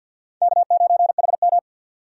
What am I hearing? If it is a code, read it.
Morse code, heard as a keyed tone, about 52 wpm, 700 Hz. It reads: K9HM
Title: USA Call Signs - 4 Characters - 2x Character Spacing - 50wpm